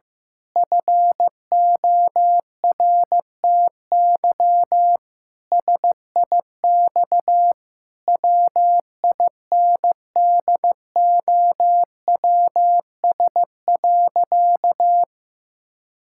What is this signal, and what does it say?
Morse code, heard as a keyed tone, about 15 wpm, 700 Hz. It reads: FORTY SIX WINDOWS.
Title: Pangram Challenge v2 - 15wpm